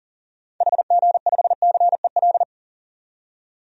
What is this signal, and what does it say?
Morse code, heard as a keyed tone, about 40 wpm, 700 Hz. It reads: HG5CEL
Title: DXCC-100 Call Signs - 6 Characters - 40wpm